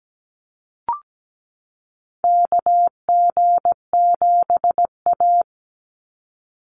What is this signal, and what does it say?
Morse code, heard as a keyed tone, about 17 wpm, 700 Hz. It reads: KG7A
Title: Call Signs - 17wpm